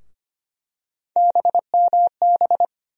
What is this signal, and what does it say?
Morse code, heard as a keyed tone, about 25 wpm, 700 Hz. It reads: BMB
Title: USA Call Sign Postfixes - 3 Letters 25wpm